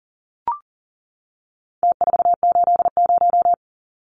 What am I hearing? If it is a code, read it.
Morse code, heard as a keyed tone, about 40 wpm, 700 Hz. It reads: T480
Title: Numbers - Contest Style - 4-Digit 40wpm